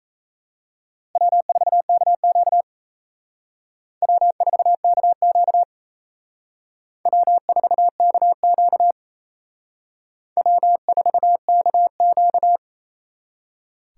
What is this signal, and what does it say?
Morse code, heard as a keyed tone, about 35 wpm, 700 Hz. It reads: W4KQ W4KQ W4KQ W4KQ
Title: Speed Racing - USA 1x2 Call Signs 28wpm